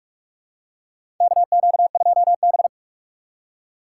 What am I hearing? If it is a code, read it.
Morse code, heard as a keyed tone, about 47 wpm, 700 Hz. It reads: KQ2B
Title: USA Call Signs - 4 Characters - 4x Character Spacing - 45wpm